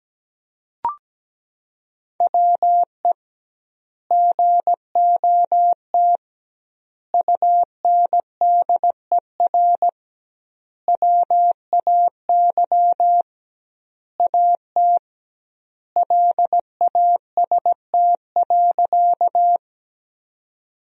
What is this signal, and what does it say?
Morse code, heard as a keyed tone, about 17 wpm, 700 Hz. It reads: WE GOT UNDER WAY AT LAST.
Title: 2x Word-Spacing - Sentences from Top 300 Words - 17wpm